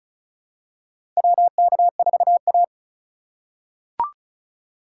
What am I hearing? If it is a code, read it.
Morse code, heard as a keyed tone, about 36 wpm, 700 Hz. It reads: WK4A
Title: USA Call Signs - 4 Characters - 4x Character Spacing - 35wpm